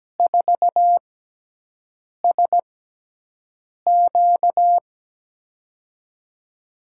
Morse code, heard as a keyed tone, about 17 wpm, spaced as for 6 wpm, 700 Hz. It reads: 4SQ
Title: USA Call Signs - 6x Character Spacing - 17wpm